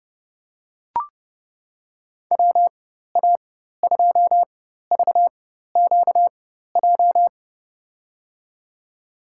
Morse code, heard as a keyed tone, about 30 wpm, spaced as for 14 wpm, 700 Hz. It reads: WA2VQJ
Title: USA Call Signs - 6 Characters - 4x Character Spacing - 30wpm